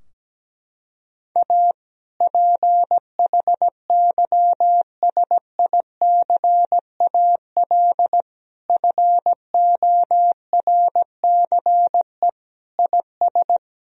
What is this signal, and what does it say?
Morse code, heard as a keyed tone, about 17 wpm, 700 Hz. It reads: A PHYSICAL FORCE IS
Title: Sentences from Top 1000 Words - Review Part 4 - 17wpm